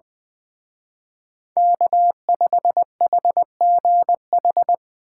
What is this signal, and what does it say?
Morse code, heard as a keyed tone, about 20 wpm, 700 Hz. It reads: K5HGH
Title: Speed Racing - USA Call Signs 17wpm